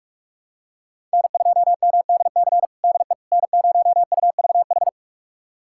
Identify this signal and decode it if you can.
Morse code, heard as a keyed tone, about 42 wpm, 700 Hz. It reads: N2MDC DE N0UVH
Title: USA Callsign DE Callsign - 5 Characters - 45wpm